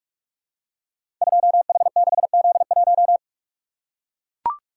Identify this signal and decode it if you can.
Morse code, heard as a keyed tone, about 44 wpm, 700 Hz. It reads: 2H6Z1
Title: Code Groups - 5 Letter:Numbers - 45wpm